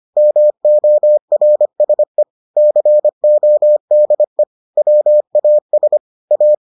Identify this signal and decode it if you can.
Morse code, heard as a keyed tone, about 25 wpm, 600 Hz. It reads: MORSE CODE WAS A